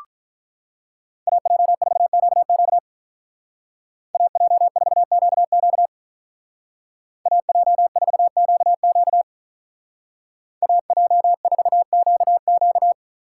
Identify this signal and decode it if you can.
Morse code, heard as a keyed tone, about 43 wpm, 700 Hz. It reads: AJ4QQ AJ4QQ AJ4QQ AJ4QQ
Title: Speed Racing - USA 2x2 Call Signs 35wpm